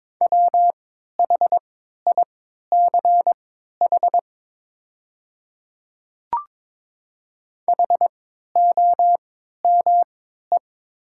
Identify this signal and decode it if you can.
Morse code, heard as a keyed tone, about 22 wpm, spaced as for 12 wpm, 690 Hz. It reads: WHICH HOME
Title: 3x Character Spacing - Top 200 Words - 22wpm